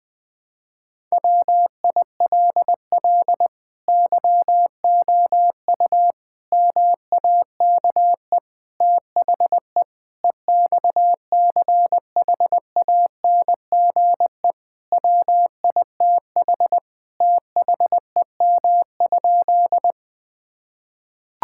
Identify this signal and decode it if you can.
Morse code, heard as a keyed tone, about 20 wpm, 700 Hz. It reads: WILL YOU MAKE THE EXCHANGE WITH THEM?
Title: Sentences from Top 1000 Words - Review Part 6 - 20wpm